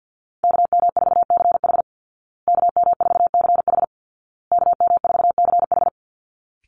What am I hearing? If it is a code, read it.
Morse code, heard as a keyed tone, about 50 wpm, 700 Hz. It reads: KM4CH KM4CH KM4CH
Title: Call Signs Repeated 2x - 50wpm